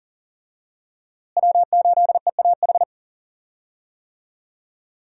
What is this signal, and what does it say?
Morse code, heard as a keyed tone, about 40 wpm, 700 Hz. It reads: W8EAH
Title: USA Call Signs - 5 Characters - Repeated 2x - 40wpm